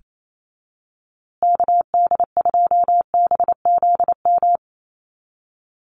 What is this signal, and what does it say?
Morse code, heard as a keyed tone, about 28 wpm, 700 Hz. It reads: KD2BZM
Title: USA Call Signs Repeated 2x 28wpm